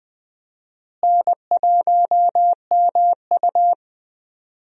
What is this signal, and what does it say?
Morse code, heard as a keyed tone, about 21 wpm, 700 Hz. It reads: N1MU
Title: USA Call Signs - 4 Characters - Repeated 2x - 20wpm